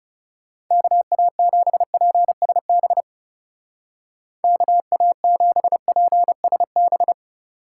Morse code, heard as a keyed tone, about 36 wpm, 700 Hz. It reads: KA7PSB KA7PSB
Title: Speed Racing - USA Call Signs 30wpm